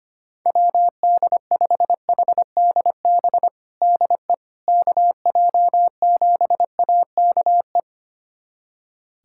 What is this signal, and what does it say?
Morse code, heard as a keyed tone, about 25 wpm, 700 Hz. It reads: WD5HDB DE KJ7AKE
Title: USA Callsign DE Callsign - 6 Characters - 25wpm